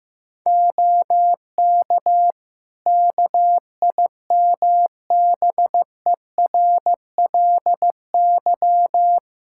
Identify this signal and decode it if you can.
Morse code, heard as a keyed tone, about 15 wpm, 700 Hz. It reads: OK KIMBERLY
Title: Practice QSO - Exchange 4 - Ending - 15wpm